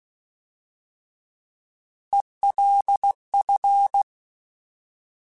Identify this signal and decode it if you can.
Morse code, heard as a keyed tone, about 16 wpm, 800 Hz. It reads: ELF
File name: Alphabetical / Alphabetical mp3s / Es @15